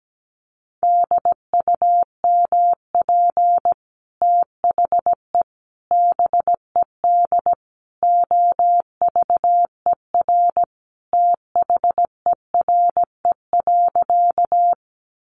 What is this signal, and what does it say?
Morse code, heard as a keyed tone, about 17 wpm, 700 Hz. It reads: DUMP THE BED OVER THERE.
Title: Sentences from Top 1300 Words 17wpm